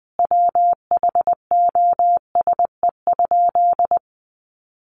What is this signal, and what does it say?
Morse code, heard as a keyed tone, about 20 wpm, 700 Hz. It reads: WHOSE?
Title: Sentences from Top 1200 Words 20wpm